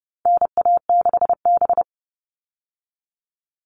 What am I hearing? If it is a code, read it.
Morse code, heard as a keyed tone, about 30 wpm, 700 Hz. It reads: NA6B
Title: All Cut Numbers - 4-Digit 30wpm